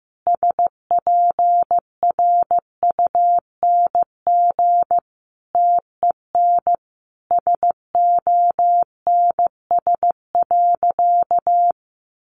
Morse code, heard as a keyed tone, about 15 wpm, 700 Hz. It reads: SPRUNG TEN SONS.
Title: A Sound Of Thunder 15wpm